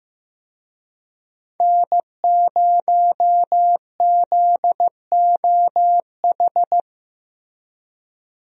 Morse code, heard as a keyed tone, about 15 wpm, 700 Hz. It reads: N0ZOH
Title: USA Call Signs - 5 Characters - Repeated 2x - 15wpm